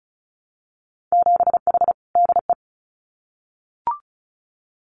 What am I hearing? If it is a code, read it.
Morse code, heard as a keyed tone, about 34 wpm, 700 Hz. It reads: 7H DE
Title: Winter Field Day Exchanges Encore 35wpm